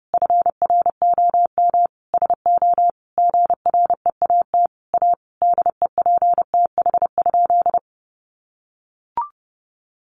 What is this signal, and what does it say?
Morse code, heard as a keyed tone, about 30 wpm, 700 Hz. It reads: FROM SO GREAT A DEPTH?
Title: Sentences from Top 1200 Words 30wpm